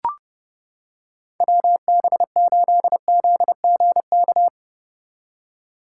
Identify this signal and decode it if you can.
Morse code, heard as a keyed tone, about 30 wpm, 700 Hz. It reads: WB8ZGK